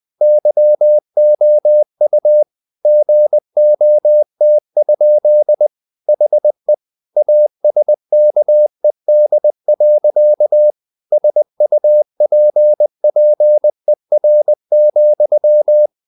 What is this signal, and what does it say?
Morse code, heard as a keyed tone, about 20 wpm, 600 Hz. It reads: YOU GOT? HE ASKED. SUPPER,
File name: the old man and the sea / the old man and the sea part 1 20wpm